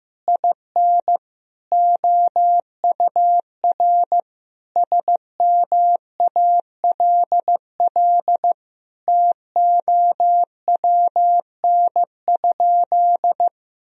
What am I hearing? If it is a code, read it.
Morse code, heard as a keyed tone, about 15 wpm, 700 Hz. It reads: IN OUR SMALL TOWN?
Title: Sentences from Top 1000 Words - Review Part 4 - 15wpm